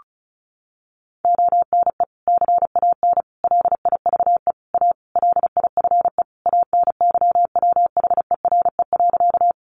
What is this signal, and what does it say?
Morse code, heard as a keyed tone, about 34 wpm, 700 Hz. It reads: ONE CAN LIVE A LIFE ANYWHERE.
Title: Sentences from Top 1400 Words 35wpm